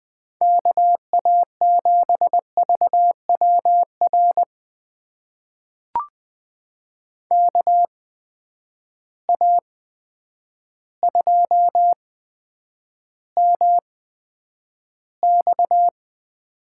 Morse code, heard as a keyed tone, about 20 wpm, spaced as for 9 wpm, 700 Hz. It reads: KA7VWR KA2MX
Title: USA Call Signs - 6 Characters - 8x Character Spacing - 20wpm